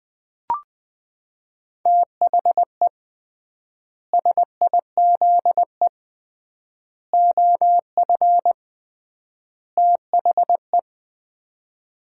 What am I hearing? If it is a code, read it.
Morse code, heard as a keyed tone, about 21 wpm, 700 Hz. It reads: THE SIZE OF THE
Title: 3x Word-Spacing - Sets of 4 Words - 20wpm